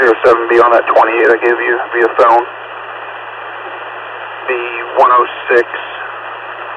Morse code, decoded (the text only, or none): none